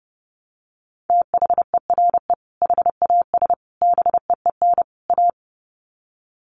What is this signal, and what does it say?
Morse code, heard as a keyed tone, about 30 wpm, 700 Hz. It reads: THERE HAS BEEN A
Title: Sets of 4 Words 30wpm